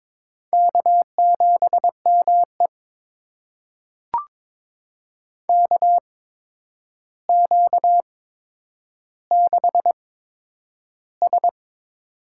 Morse code, heard as a keyed tone, about 22 wpm, spaced as for 9 wpm, 700 Hz. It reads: K7ME KQ6S
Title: USA Call Signs - 4 Characters - 8x Character Spacing - 22wpm